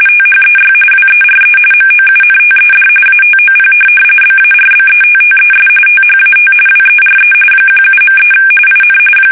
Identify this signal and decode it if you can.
RTTY, about 45.45 baud. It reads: THEY WERE MOUNTED ATOP BARBETTES THAT WERE 17.3 IN
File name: fsk850 / USS-Massachusetts